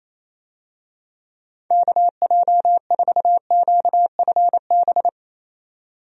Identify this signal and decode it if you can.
Morse code, heard as a keyed tone, about 28 wpm, 700 Hz. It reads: KJ4QFB
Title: USA Call Signs - 6 Characters - 28wpm